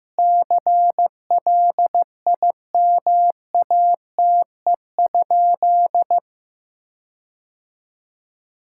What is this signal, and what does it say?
Morse code, heard as a keyed tone, about 16 wpm, 700 Hz. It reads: CLIMATE?
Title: Sentences from Top 1600 Words 15wpm